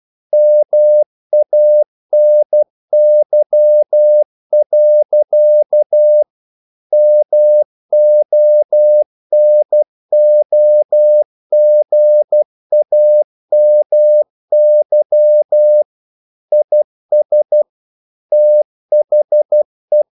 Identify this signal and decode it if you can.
Morse code, heard as a keyed tone, about 12 wpm, 600 Hz. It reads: MANY. MONOGAMY IS THE
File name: CW / Practice / EnglishQuotations0312